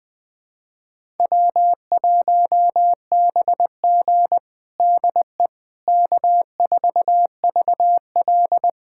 Morse code, heard as a keyed tone, about 21 wpm, 700 Hz. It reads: W1BG DE K4VL
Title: USA Callsign DE Callsign - 4 Characters - 20wpm